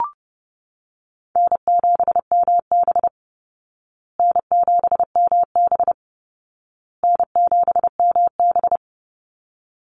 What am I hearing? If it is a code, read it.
Morse code, heard as a keyed tone, about 30 wpm, 700 Hz. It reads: N7MB N7MB N7MB
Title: Easy Call Signs Repeated 3x - 30wpm